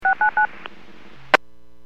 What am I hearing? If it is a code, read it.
DTMF, heard as telephone touch tones, tones of about 110 ms. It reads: 699
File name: WPS / 00313 18-31 WPS 451.25 WPS